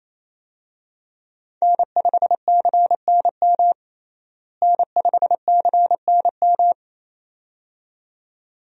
Morse code, heard as a keyed tone, about 27 wpm, 700 Hz. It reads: N5CNM N5CNM
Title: USA Call Signs - 5 Characters - Repeated 2x - 28wpm